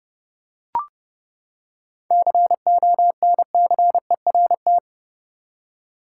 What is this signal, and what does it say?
Morse code, heard as a keyed tone, about 30 wpm, 700 Hz. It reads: CONCERT
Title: Seven Letter Words - 30wpm